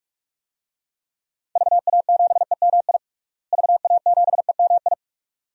Morse code, heard as a keyed tone, about 41 wpm, 700 Hz. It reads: VA7EMI VA7EMI
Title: DXCC-100 Call Signs - 6 Characters - Repeated 2x - 45wpm